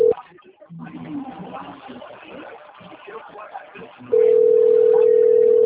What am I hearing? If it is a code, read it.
Ringback tone (North America), 440 + 480 Hz, with speech.
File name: rg-1600-unknown- - - .18481